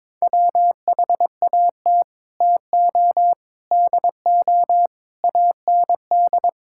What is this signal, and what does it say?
Morse code, heard as a keyed tone, about 22 wpm, 700 Hz. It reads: WHAT TO DO AND
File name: practice / 22wpm / Sets of 8 Words - 22wpm